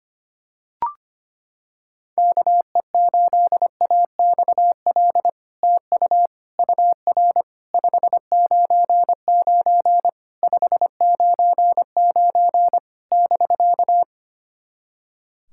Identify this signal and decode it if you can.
Morse code, heard as a keyed tone, about 25 wpm, 700 Hz. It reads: KE8AXL TU UR 599 599 <BK>
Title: SOTA QSO Practice 25wpm